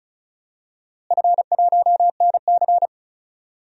Morse code, heard as a keyed tone, about 36 wpm, 700 Hz. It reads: F1NC
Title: DXCC-100 Call Signs - 4 Characters - Repeated 2x - 35wpm